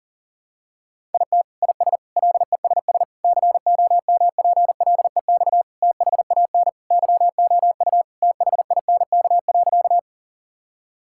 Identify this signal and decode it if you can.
Morse code, heard as a keyed tone, about 40 wpm, 700 Hz. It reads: IT IS LESS COMPLEX THAN YOU THINK.
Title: Sentences from Top 1000 Words - Review Part 6 - 40wpm